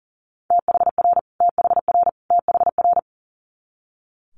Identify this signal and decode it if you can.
Morse code, heard as a keyed tone, about 40 wpm, 690 Hz. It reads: THR THR THR
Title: Multi-letter Phonemes Repeated 3x 40wpm